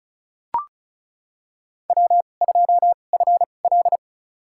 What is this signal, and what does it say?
Morse code, heard as a keyed tone, about 34 wpm, spaced as for 25 wpm, 700 Hz. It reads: W2FL
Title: USA Call Signs - 4 Characters - 2x Character Spacing - 35wpm